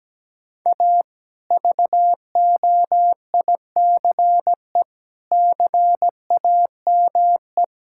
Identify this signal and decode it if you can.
Morse code, heard as a keyed tone, about 17 wpm, 700 Hz. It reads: A VOICE CAME
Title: Sentences from Top 1000 Words - Review Part 6 - 17wpm